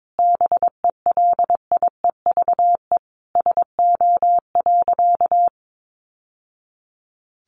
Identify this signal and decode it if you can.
Morse code, heard as a keyed tone, about 22 wpm, 700 Hz. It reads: BELIEVE SO.
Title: Sentences from Top 1000 Words 22wpm